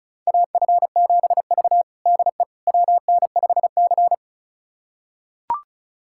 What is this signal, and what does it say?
Morse code, heard as a keyed tone, about 36 wpm, 700 Hz. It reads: AF7V DE WN5C
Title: USA Callsign DE Callsign - 4 Characters - 35wpm